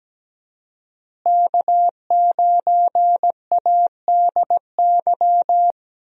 Morse code, heard as a keyed tone, about 17 wpm, 700 Hz. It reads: K9ADY
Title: USA Call Signs - 5 Characters - Repeated 3x - 17wpm